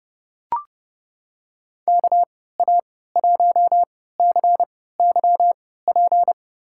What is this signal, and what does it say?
Morse code, heard as a keyed tone, about 29 wpm, spaced as for 17 wpm, 700 Hz. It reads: KA1CYP
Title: USA Call Signs - 6 Characters - 3x Character Spacing - 30wpm